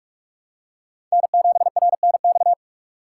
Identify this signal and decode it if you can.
Morse code, heard as a keyed tone, about 46 wpm, 700 Hz. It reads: N7RNX
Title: USA Call Signs - 5 Characters - 4x Character Spacing - 45wpm